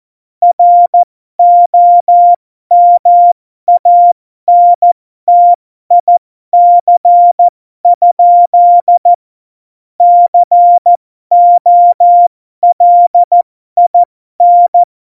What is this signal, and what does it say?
Morse code, heard as a keyed tone, about 14 wpm, spaced as for 12 wpm, 700 Hz. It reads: ROMANTIC? COLIN
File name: Morsekurs / Witze